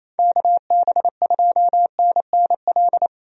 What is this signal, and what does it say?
Morse code, heard as a keyed tone, about 28 wpm, 700 Hz. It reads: KB2NNL
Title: USA Call Signs - 6 Characters - 28wpm